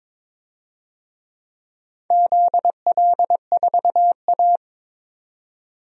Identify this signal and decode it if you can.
Morse code, heard as a keyed tone, about 22 wpm, 700 Hz. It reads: ZL4A
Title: DXCC-100 Call Signs - 4 Characters - 22wpm